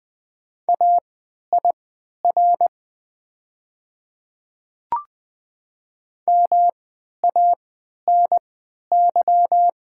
Morse code, heard as a keyed tone, about 20 wpm, spaced as for 11 wpm, 700 Hz. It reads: AIR MANY
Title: 3x Character Spacing - Top 200 Words - 20wpm